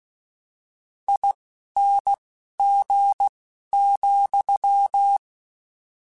Morse code, heard as a keyed tone, about 16 wpm, 800 Hz. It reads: I N G ,
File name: General / General-mp3s / ESE ING @10